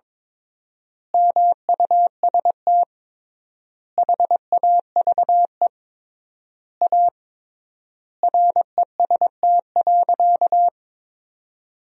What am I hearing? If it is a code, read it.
Morse code, heard as a keyed tone, about 22 wpm, 700 Hz. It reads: MUST HAVE A REST.
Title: 3x Word-Spacing - Sentences from Top 500 Words - 22wpm